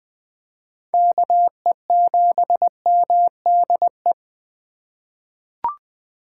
Morse code, heard as a keyed tone, about 20 wpm, 700 Hz. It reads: KE7MDE
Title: USA Call Signs - 4x Character Spacing - 20wpm